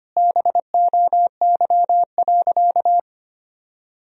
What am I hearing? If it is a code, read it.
Morse code, heard as a keyed tone, about 25 wpm, 700 Hz. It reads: BOY.
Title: Sentences from Top 1000 Words - Review Part 1 - 25wpm